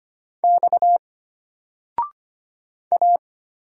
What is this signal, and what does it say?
Morse code, heard as a keyed tone, about 25 wpm, 700 Hz. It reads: X A
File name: Single-Letter-Number-25wpm